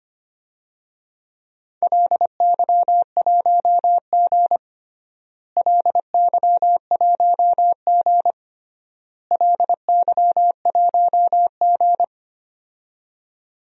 Morse code, heard as a keyed tone, about 25 wpm, 700 Hz. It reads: LY1G LY1G LY1G
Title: DXCC-100 Call Signs - 4 Characters - Repeated 3x - 25wpm